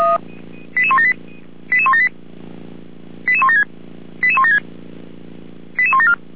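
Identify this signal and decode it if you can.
Five-tone selective call (ZVEI), heard as quick runs of five tones, tones of about 70 ms. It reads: 88178 88178 88176 88176 88174
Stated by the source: ZVEI five-tone selective call 88352 received on an SDR.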